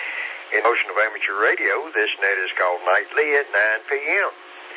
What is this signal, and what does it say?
Emergency Alert System (SAME) tones, heard as harsh buzzing bursts, with speech. It is heard from 0.8 to 3.8 s.